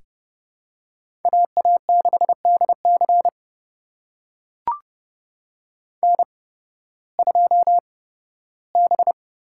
Morse code, heard as a keyed tone, about 30 wpm, spaced as for 16 wpm, 700 Hz. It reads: AA6DC N2B
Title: USA Call Signs - 5 Characters - 8x Character Spacing - 30wpm